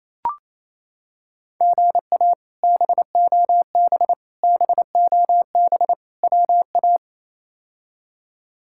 Morse code, heard as a keyed tone, about 28 wpm, 700 Hz. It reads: GA BOB BOB WA
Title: K1USN SST QSO Practice - 28wpm